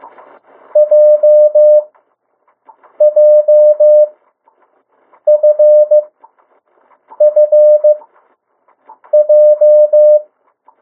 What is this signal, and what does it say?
Morse code, heard as a keyed tone, about 15 wpm, spaced as for 6 wpm, 600 Hz. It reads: JJFFJ
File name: morse mp3 / 02 urok 1